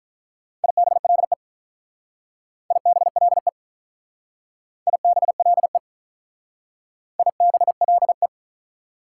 Morse code, heard as a keyed tone, about 47 wpm, 690 Hz. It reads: IBLE IBLE IBLE IBLE